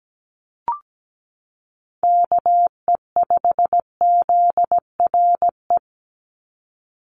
Morse code, heard as a keyed tone, about 17 wpm, 700 Hz. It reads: KE5ZRE
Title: USA Call Signs 17wpm